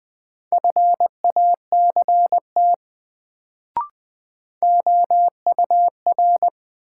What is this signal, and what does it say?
Morse code, heard as a keyed tone, about 20 wpm, 700 Hz. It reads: FACT OUR
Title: Sets of 2 Words - New Word Drill - 20wpm